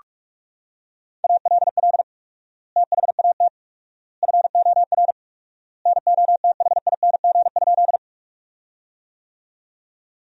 Morse code, heard as a keyed tone, about 46 wpm, 700 Hz. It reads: ALL THAT FOR NOTHING?
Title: 4x Word-Spacing - Sentences from Top 500 Words - 45wpm